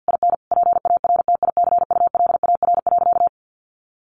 Morse code, heard as a keyed tone, about 50 wpm, 700 Hz. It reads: IN PARTICULAR.
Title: Sentences from Top 900 Words 50wpm